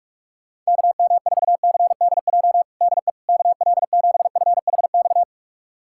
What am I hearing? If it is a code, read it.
Morse code, heard as a keyed tone, about 46 wpm, 700 Hz. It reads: KM4CDJ DE KL7FHX
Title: USA Callsign DE Callsign - 6 Characters - 45wpm